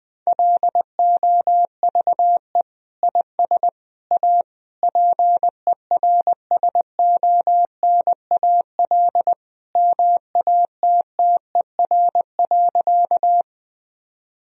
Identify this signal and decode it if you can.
Morse code, heard as a keyed tone, about 21 wpm, 700 Hz. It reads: LOVE IS A PERSONAL MATTER.
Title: Sentences from Top 1000 Words - Review Part 4 - 20wpm